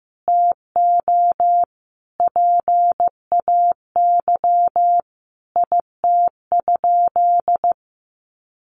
Morse code, heard as a keyed tone, about 15 wpm, 700 Hz. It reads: TO PAY IT?
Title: Sentences from Top 1100 Words 15wpm